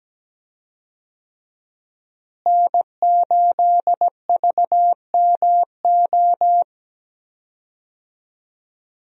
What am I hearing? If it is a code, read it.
Morse code, heard as a keyed tone, about 17 wpm, 700 Hz. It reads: N8VMO
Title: USA Call Signs - 5 Characters - Repeated 3x - 17wpm